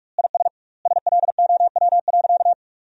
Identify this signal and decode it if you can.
Morse code, heard as a keyed tone, about 41 wpm, 700 Hz. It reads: IS SLOW.
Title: 2x Word-Spacing - Sentences from Top 400 Words - 45wpm